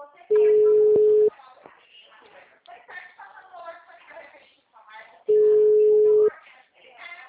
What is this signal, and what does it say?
Ringback tone (Europe, 425 Hz), with speech.
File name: out- -2005- - - .54